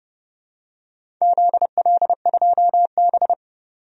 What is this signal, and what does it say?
Morse code, heard as a keyed tone, about 30 wpm, 700 Hz. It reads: ZL2B
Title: DXCC-100 Call Signs - 4 Characters - 30wpm